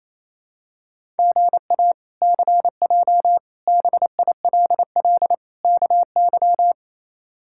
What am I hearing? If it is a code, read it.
Morse code, heard as a keyed tone, about 28 wpm, 700 Hz. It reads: GA CJ BILL KY
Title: K1USN SST QSO Practice - 28wpm